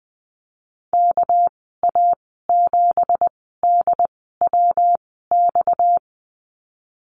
Morse code, heard as a keyed tone, about 19 wpm, spaced as for 14 wpm, 700 Hz. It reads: KA7DWX
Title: 231 Lesson 37g - Letter x - Call Signs 20wpm